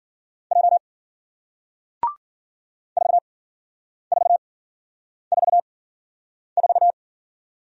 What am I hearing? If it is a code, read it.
Morse code, heard as a keyed tone, about 54 wpm, spaced as for 9 wpm, 700 Hz. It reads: J 4444